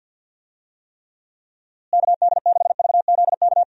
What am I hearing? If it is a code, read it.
Morse code, heard as a keyed tone, about 52 wpm, 700 Hz. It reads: KD6VZK
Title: USA Call Signs - 6 Characters - Repeated 3x - 50wpm